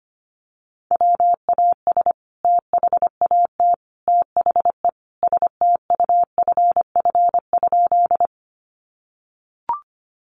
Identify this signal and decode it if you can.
Morse code, heard as a keyed tone, about 25 wpm, 700 Hz. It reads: WAS THAT THE STUFF?
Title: Sentences from Top 1400 Words 25wpm